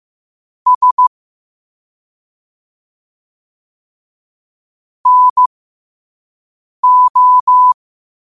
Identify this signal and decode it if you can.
Morse code, heard as a keyed tone, about 15 wpm, spaced as for 5 wpm, 1000 Hz. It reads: S NO
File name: qotdfiles / 5wpm